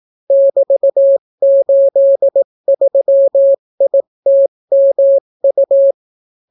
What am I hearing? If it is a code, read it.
Morse code, heard as a keyed tone, about 18 wpm, 550 Hz. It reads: <BT>83ITMU